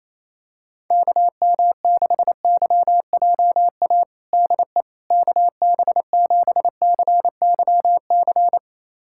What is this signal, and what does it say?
Morse code, heard as a keyed tone, about 28 wpm, 700 Hz. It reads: KM6YJA DE KB7CYC
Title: USA Callsign DE Callsign - 6 Characters - 28wpm